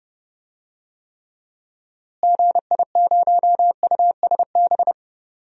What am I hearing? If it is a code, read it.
Morse code, heard as a keyed tone, about 30 wpm, 700 Hz. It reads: GI0USB
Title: DXCC-100 Call Signs - 6 Characters - Repeated 3x - 30wpm